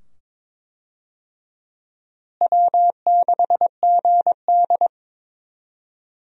Morse code, heard as a keyed tone, about 22 wpm, 700 Hz. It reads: W6GD
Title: USA Call Signs - 4 Characters - Repeated 2x - 22wpm